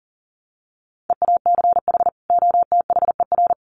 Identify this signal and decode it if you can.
Morse code, heard as a keyed tone, about 40 wpm, 700 Hz. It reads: EACH OTHER
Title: Sets of 2 Words 40wpm